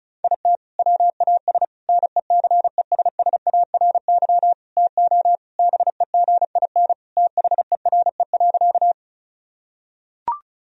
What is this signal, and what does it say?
Morse code, heard as a keyed tone, about 36 wpm, 700 Hz. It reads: IT WAS NECESSARY TO BEGIN THERE.
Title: Sentences from Top 1000 Words - Review Part 1 - 35wpm